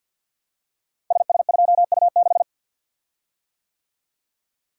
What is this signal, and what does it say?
Morse code, heard as a keyed tone, about 53 wpm, 700 Hz. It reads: SS2U6
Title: Code Groups - 5 Letter:Numbers - 50wpm